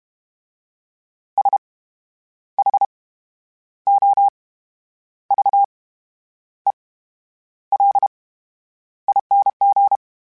Morse code, heard as a keyed tone, about 32 wpm, 800 Hz. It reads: S H O V E L ING
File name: Special INGs / Special INGs-mp3s / VerySlowToFaster Ing @30